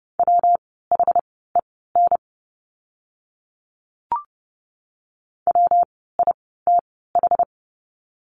Morse code, heard as a keyed tone, about 30 wpm, spaced as for 17 wpm, 700 Hz. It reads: WHEN WITH
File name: course / 30:10wpm / 237 Lesson 38f - Letter q - Words 30wpm